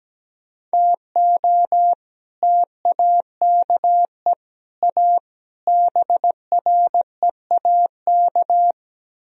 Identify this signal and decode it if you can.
Morse code, heard as a keyed tone, about 19 wpm, 700 Hz. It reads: TO TAKE A BREAK
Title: Speed-Racing - Sets of 4 Words 17wpm